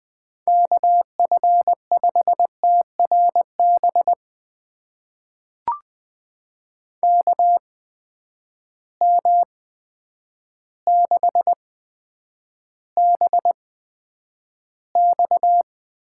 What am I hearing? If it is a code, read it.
Morse code, heard as a keyed tone, about 20 wpm, spaced as for 9 wpm, 700 Hz. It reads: KF5TRB KM6BX
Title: USA Call Signs - 6 Characters - 8x Character Spacing - 20wpm